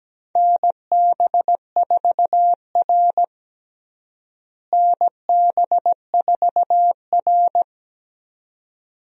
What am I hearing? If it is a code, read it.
Morse code, heard as a keyed tone, about 17 wpm, 700 Hz. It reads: NB4R NB4R
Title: USA Call Signs - 4 Characters - Repeated 2x - 17wpm